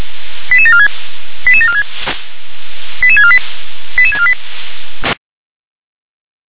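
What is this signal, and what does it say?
Five-tone selective call (ZVEI), heard as quick runs of five tones, tones of about 70 ms. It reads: 88746 88746 88648 88648
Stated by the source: ZVEI five-tone selective call received by SDR.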